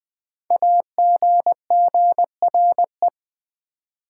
Morse code, heard as a keyed tone, about 20 wpm, 700 Hz. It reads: AGGRE
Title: Build A Word - 5 Letters At A Time - 10-Letter Words - 20wpm